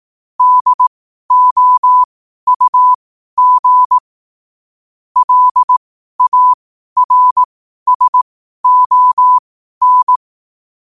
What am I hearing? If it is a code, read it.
Morse code, heard as a keyed tone, about 18 wpm, spaced as for 12 wpm, 1000 Hz. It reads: DOUG LARSON
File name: qotdfiles / 12wpm